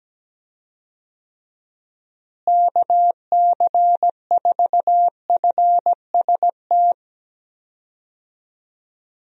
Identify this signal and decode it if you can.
Morse code, heard as a keyed tone, about 17 wpm, 700 Hz. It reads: KC4FST
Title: USA Call Signs - 6 Characters - Repeated 2x - 17wpm